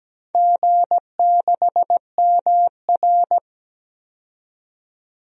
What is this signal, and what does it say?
Morse code, heard as a keyed tone, about 17 wpm, 700 Hz. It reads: G6MR
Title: DXCC-100 Call Signs - 4 Characters - 17wpm